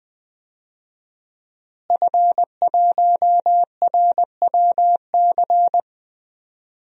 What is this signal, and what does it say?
Morse code, heard as a keyed tone, about 20 wpm, 700 Hz. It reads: F1RWC
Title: DXCC-100 Call Signs - 5 Characters - Repeated 3x - 20wpm